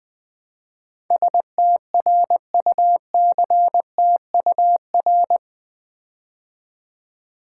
Morse code, heard as a keyed tone, about 20 wpm, 700 Hz. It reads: STRUCTUR
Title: Build A Word - 4 Letters At A Time - 10-Letter Words - 20wpm